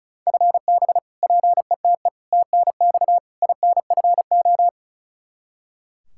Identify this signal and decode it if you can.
Morse code, heard as a keyed tone, about 34 wpm, 700 Hz. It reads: FB PETE TNX INFO
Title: Practice QSO - Exchange 3 - More Info - 35wpm